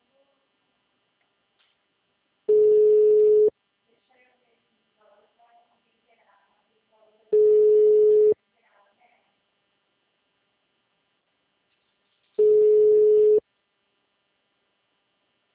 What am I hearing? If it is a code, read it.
Ringback tone (Europe, 425 Hz), with speech.